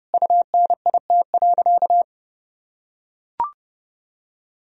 Morse code, heard as a keyed tone, about 30 wpm, 700 Hz. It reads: UNIT.
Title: Sentences from Top 1000 Words - Review Part 2 - 30wpm